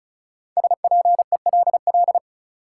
Morse code, heard as a keyed tone, about 36 wpm, 700 Hz. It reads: SPELL
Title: Sentences from Top 1000 Words - Review Part 1 - New Word Drill - 35wpm